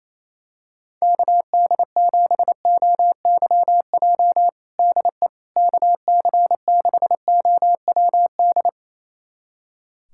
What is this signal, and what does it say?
Morse code, heard as a keyed tone, about 28 wpm, 700 Hz. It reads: KD7OYJ DE KC6OWD
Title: USA Callsign DE Callsign - 6 Characters - 28wpm